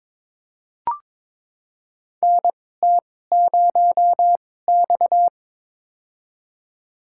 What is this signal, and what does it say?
Morse code, heard as a keyed tone, about 22 wpm, spaced as for 16 wpm, 700 Hz. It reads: NT0X
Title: USA Call Signs - 4 Characters - 2x Character Spacing - 22wpm